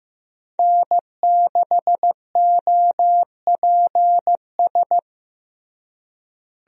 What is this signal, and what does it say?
Morse code, heard as a keyed tone, about 16 wpm, 700 Hz. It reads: N6OPS
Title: USA Call Signs - 5 Characters - Repeated 2x - 15wpm